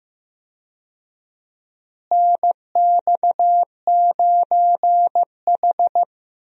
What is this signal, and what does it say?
Morse code, heard as a keyed tone, about 15 wpm, 700 Hz. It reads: NX9H
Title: USA Call Signs - 4 Characters - Repeated 3x - 15wpm